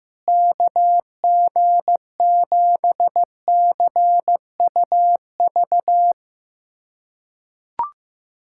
Morse code, heard as a keyed tone, about 15 wpm, 700 Hz. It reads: KG7CUV
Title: USA Callsign DE Callsign - 6 Characters - 15wpm